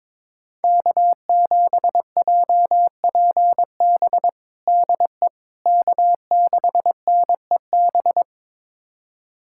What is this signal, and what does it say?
Morse code, heard as a keyed tone, about 21 wpm, 700 Hz. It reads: K7JPB DE K6NEB
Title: USA Callsign DE Callsign 22wpm